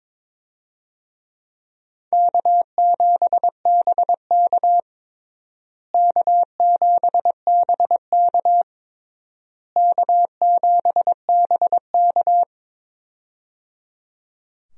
Morse code, heard as a keyed tone, about 22 wpm, 700 Hz. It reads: K7BK K7BK K7BK
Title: USA Call Signs - 4 Characters - Repeated 3x - 22wpm